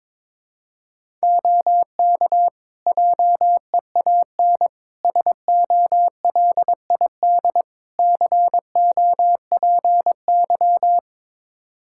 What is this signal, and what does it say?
Morse code, heard as a keyed tone, about 22 wpm, 700 Hz. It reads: OK JEAN SOLID COPY
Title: Practice QSO - Exchange 3 - More Info - 22wpm